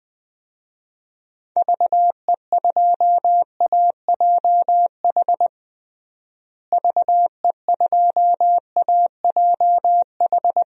Morse code, heard as a keyed tone, about 20 wpm, 700 Hz. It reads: VE2AJH VE2AJH
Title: DXCC-100 Call Signs - 6 Characters - Repeated 2x - 20wpm